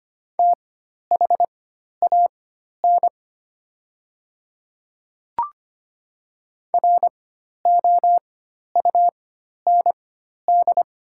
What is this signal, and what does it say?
Morse code, heard as a keyed tone, about 25 wpm, spaced as for 12 wpm, 690 Hz. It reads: THAN ROUND
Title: 4x Character Spacing - Top 200 Words - 25wpm